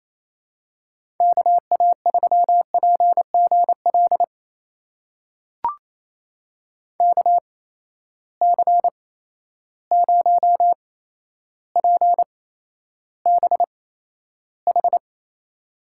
Morse code, heard as a keyed tone, about 28 wpm, spaced as for 12 wpm, 700 Hz. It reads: KA3PGL KC0PBH
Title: USA Call Signs - 6 Characters - 8x Character Spacing - 28wpm